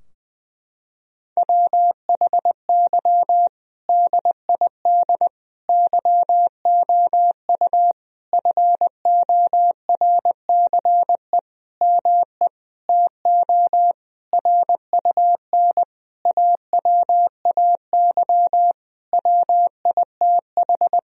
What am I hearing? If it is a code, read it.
Morse code, heard as a keyed tone, about 20 wpm, 700 Hz. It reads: WHY DID YOU FORCE ME TO RUN AWAY WITH
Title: Sentences from Top 1000 Words - Review Part 3 - 20wpm